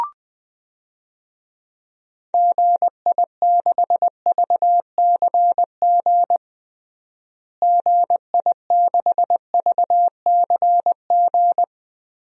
Morse code, heard as a keyed tone, about 21 wpm, 700 Hz. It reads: GI6VCG GI6VCG
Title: DXCC-100 Call Signs - 6 Characters - Repeated 2x - 20wpm